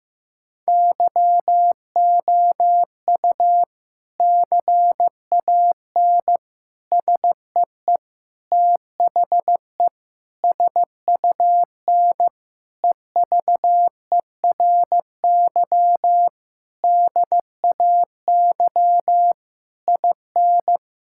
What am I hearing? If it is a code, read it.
Morse code, heard as a keyed tone, about 15 wpm, 700 Hz. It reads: YOU CAN SEE THE SUN EVERY DAY IN
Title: Sentences from Top 1600 Words 15wpm